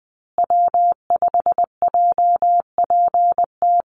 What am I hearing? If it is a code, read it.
Morse code, heard as a keyed tone, about 20 wpm, 700 Hz. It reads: W5JPT
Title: USA Call Signs Repeated 2x 20wpm